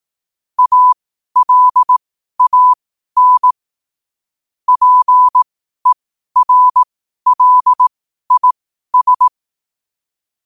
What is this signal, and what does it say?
Morse code, heard as a keyed tone, about 18 wpm, spaced as for 12 wpm, 1000 Hz. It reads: ALAN PERLIS
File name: qotdfiles / 12wpm